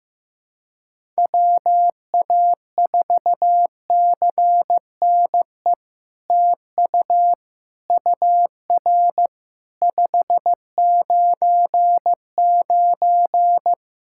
Morse code, heard as a keyed tone, about 15 wpm, 700 Hz. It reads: WA4CNE TU UR 599
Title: SOTA QSO Practice 15wpm